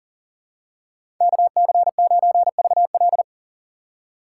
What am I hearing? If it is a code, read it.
Morse code, heard as a keyed tone, about 40 wpm, 690 Hz. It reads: KC9VL
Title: USA Call Signs - 5 Characters - 6x Character Spacing - 40wpm